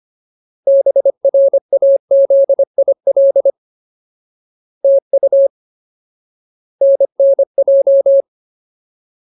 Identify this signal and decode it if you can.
Morse code, heard as a keyed tone, about 25 wpm, 550 Hz. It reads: BRAZIL TU NNJ
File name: Ultimate Word List / 25WPM Ultimate